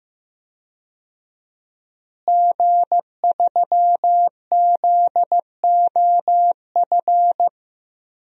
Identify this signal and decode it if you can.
Morse code, heard as a keyed tone, about 15 wpm, 700 Hz. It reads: G3ZOF
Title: DXCC-100 Call Signs - 5 Characters - Repeated 2x - 15wpm